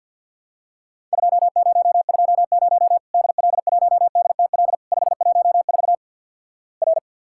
Morse code, heard as a keyed tone, about 50 wpm, 700 Hz. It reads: 2020 DL1DTL 514 R
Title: Worked All Europe DX Contest - QTC Exchange with Repeats - 50wpm